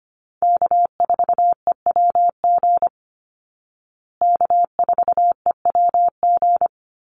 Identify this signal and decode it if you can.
Morse code, heard as a keyed tone, about 25 wpm, 700 Hz. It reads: K4EWG K4EWG
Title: Call Signs Repeated 2x - 25wpm